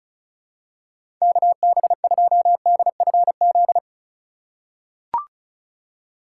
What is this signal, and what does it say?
Morse code, heard as a keyed tone, about 34 wpm, 700 Hz. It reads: KB2DFZ
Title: USA Call Signs - 6x Character Spacing - 35wpm